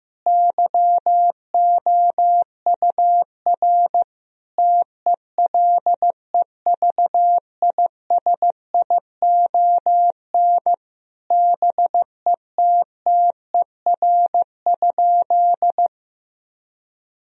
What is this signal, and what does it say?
Morse code, heard as a keyed tone, about 15 wpm, 700 Hz. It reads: YOUR TELEVISION BETTER?
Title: Sentences from Top 1000 Words - Review Part 4 - 15wpm